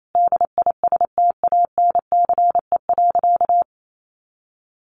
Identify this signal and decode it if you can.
Morse code, heard as a keyed tone, about 28 wpm, 700 Hz. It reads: DISTANCE.
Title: Sentences from Top 1200 Words 28wpm